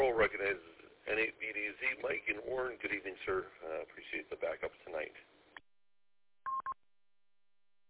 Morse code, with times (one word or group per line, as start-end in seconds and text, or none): none